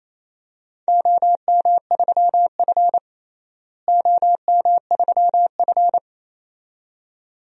Morse code, heard as a keyed tone, about 27 wpm, 700 Hz. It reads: OM3F OM3F
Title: DXCC-100 Call Signs - 4 Characters - Repeated 2x - 28wpm